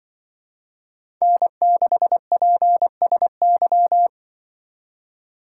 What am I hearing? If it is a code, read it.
Morse code, heard as a keyed tone, about 24 wpm, 700 Hz. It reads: N6PSY
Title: Speed Racing - USA 1x3 Call Signs 20wpm